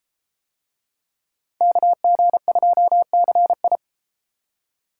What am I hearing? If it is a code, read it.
Morse code, heard as a keyed tone, about 33 wpm, 700 Hz. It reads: KG2CI